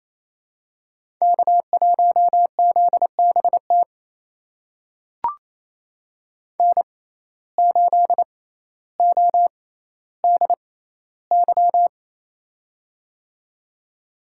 Morse code, heard as a keyed tone, about 28 wpm, spaced as for 14 wpm, 700 Hz. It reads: K1ZBT N8ODY
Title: USA Call Signs - 5 Characters - 6x Character Spacing - 28wpm